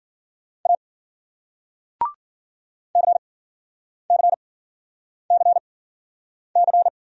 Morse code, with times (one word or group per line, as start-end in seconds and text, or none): 0.7-0.8 A
3.0-6.9 CCCC